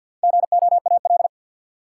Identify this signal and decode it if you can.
Morse code, heard as a keyed tone, about 51 wpm, 700 Hz. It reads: GOAL
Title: Sentences from Top 1400 Words - New Word Drill Alt - 50wpm